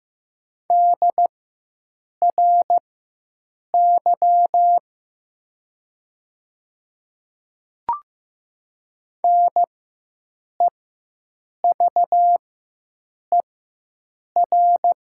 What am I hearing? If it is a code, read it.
Morse code, heard as a keyed tone, about 15 wpm, spaced as for 7 wpm, 700 Hz. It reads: DRY NEVER
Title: 4x Character Spacing - Top 500 Words - 15wpm